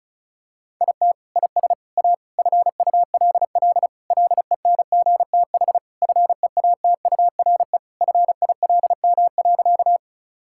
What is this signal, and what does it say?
Morse code, heard as a keyed tone, about 34 wpm, 700 Hz. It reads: IT IS A FULL LENGTH FEATURE FILM.
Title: Sentences from Top 1000 Words - Review Part 5 - 35wpm